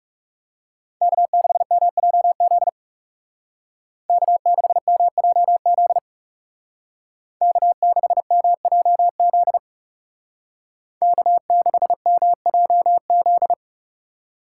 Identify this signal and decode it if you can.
Morse code, heard as a keyed tone, about 35 wpm, 700 Hz. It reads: K6MJZ K6MJZ K6MJZ K6MJZ
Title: Speed Racing - USA 1x3 Call Signs 30wpm